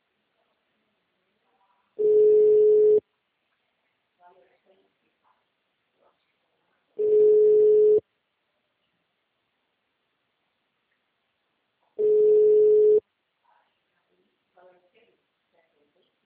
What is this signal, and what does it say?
Ringback tone (Europe, 425 Hz), with speech.